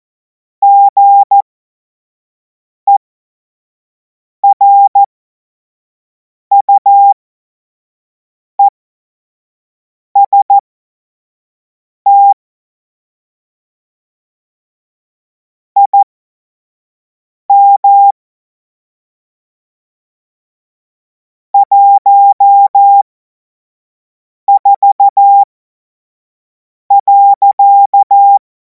Morse code, heard as a keyed tone, about 14 wpm, spaced as for 5 wpm, 800 Hz. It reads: GERUEST IM 14.
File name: Morsekurs / Witze / Witz 03. Februar 2026 25 F70